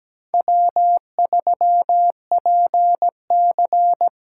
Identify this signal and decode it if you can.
Morse code, heard as a keyed tone, about 17 wpm, 700 Hz. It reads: W3PC
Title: USA Call Signs - 4 Characters - 3x Character Spacing - 17wpm